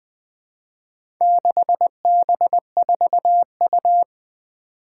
Morse code, heard as a keyed tone, about 20 wpm, 700 Hz. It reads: 6B4U
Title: Four Challenging Letter-Number Combinations 20wpm